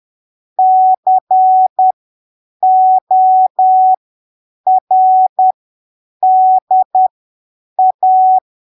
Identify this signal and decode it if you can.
Morse code, heard as a keyed tone, about 10 wpm, spaced as for 7 wpm, 750 Hz. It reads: CORDA
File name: arquivos / texto / texto 29 10ppm